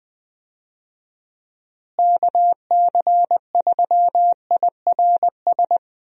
Morse code, heard as a keyed tone, about 20 wpm, 700 Hz. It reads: KC3IRS
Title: USA Call Signs - 6 Characters - Repeated 3x - 20wpm